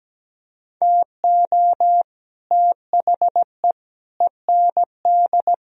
Morse code, heard as a keyed tone, about 19 wpm, 700 Hz. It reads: TO THE END
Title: Speed-Racing - Sets of 4 Words 17wpm